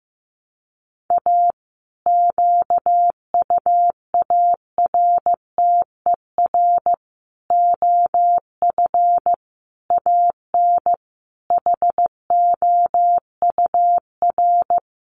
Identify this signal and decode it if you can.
Morse code, heard as a keyed tone, about 15 wpm, 700 Hz. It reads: A QUARTER OF AN HOUR
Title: Sentences from Top 1300 Words 15wpm